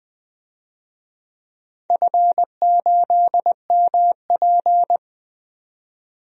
Morse code, heard as a keyed tone, about 20 wpm, 700 Hz. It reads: F8MP
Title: DXCC-100 Call Signs - 4 Characters - 20wpm